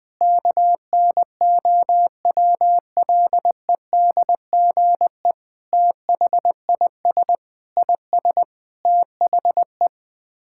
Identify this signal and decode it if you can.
Morse code, heard as a keyed tone, about 19 wpm, 700 Hz. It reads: KNOWLEDGE THIS IS THE
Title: Sets of 9 Words - 20wpm